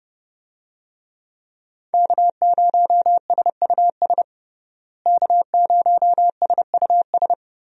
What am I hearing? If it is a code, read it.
Morse code, heard as a keyed tone, about 30 wpm, 700 Hz. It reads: K0SUS K0SUS
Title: USA Call Signs - 5 Characters - Repeated 2x - 30wpm